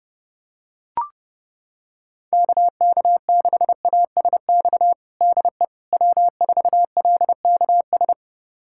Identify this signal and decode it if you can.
Morse code, heard as a keyed tone, about 30 wpm, 700 Hz. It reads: KK6ASX DE W4LKS
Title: USA Callsign DE Callsign 30wpm